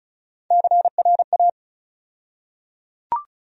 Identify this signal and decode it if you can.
Morse code, heard as a keyed tone, about 34 wpm, 700 Hz. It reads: CRA
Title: USA Call Sign Postfixes - 3 Letters 35wpm